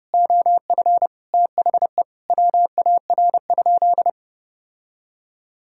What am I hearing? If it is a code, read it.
Morse code, heard as a keyed tone, about 30 wpm, 700 Hz. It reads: OF THE WAR?
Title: Sentences from Top 1000 Words - Review Part 6 - 30wpm